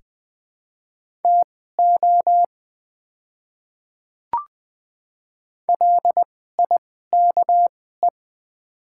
Morse code, heard as a keyed tone, about 20 wpm, spaced as for 14 wpm, 700 Hz. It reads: TO LIKE
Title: 2x Character Spacing - Top 200 Words - 20wpm